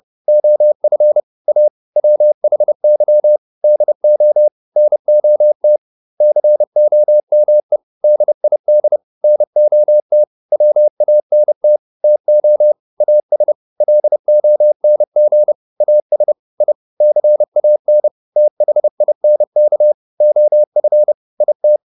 Morse code, heard as a keyed tone, about 30 wpm, 600 Hz. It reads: OF A WHY DO NOT COME DID NOT WANT TO AS LONG AS I CAN THINK OF IT